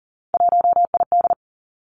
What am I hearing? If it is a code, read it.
Morse code, heard as a keyed tone, about 38 wpm, 700 Hz. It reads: 1ID
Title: Three Challenging Letter-Number Combinations 40wpm